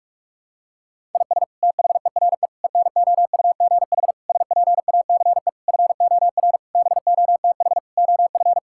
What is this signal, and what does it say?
Morse code, heard as a keyed tone, about 46 wpm, 700 Hz. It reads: IS THERE ENOUGH SPACE FOR BOTH OF
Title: Sentences from Top 1000 Words - Review Part 5 - 45wpm